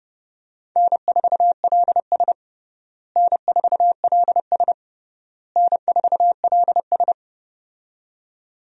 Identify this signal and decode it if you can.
Morse code, heard as a keyed tone, about 30 wpm, 700 Hz. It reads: N4LS N4LS N4LS
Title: USA Call Signs - 4 Characters - Repeated 3x - 30wpm